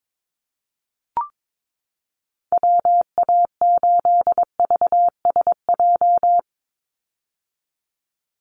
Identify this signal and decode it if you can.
Morse code, heard as a keyed tone, about 22 wpm, 700 Hz. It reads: WA8VSJ
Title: Call Signs - 22wpm